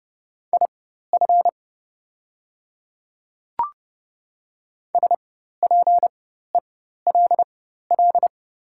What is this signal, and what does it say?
Morse code, heard as a keyed tone, about 30 wpm, spaced as for 14 wpm, 700 Hz. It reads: IF SPELL
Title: 4x Character Spacing - Top 200 Words - 30wpm